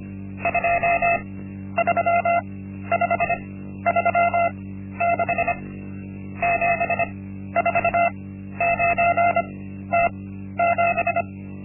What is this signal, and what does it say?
Morse code, heard as a keyed tone, about 25 wpm, spaced as for 12 wpm, 690 Hz. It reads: 23536 749T7
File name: Linksmakalnis-1